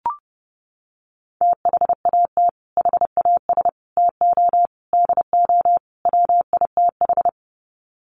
Morse code, heard as a keyed tone, about 28 wpm, 700 Hz. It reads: THAT HAS TO DO WITH
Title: Sets of 5 Words Encore2 30wpm